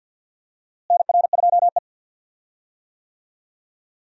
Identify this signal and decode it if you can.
Morse code, heard as a keyed tone, about 52 wpm, 700 Hz. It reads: NR2E
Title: USA Call Signs - 4 Characters - Repeated 3x - 50wpm